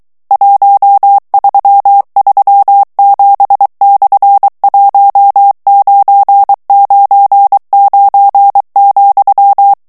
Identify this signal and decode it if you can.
Morse code, heard as a keyed tone, about 23 wpm, 800 Hz. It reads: 1337/1999,
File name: morse mp3 / ESPAGNOL TEXTE EN CLAIR 2 VITESSE 22 MOTS MINUTES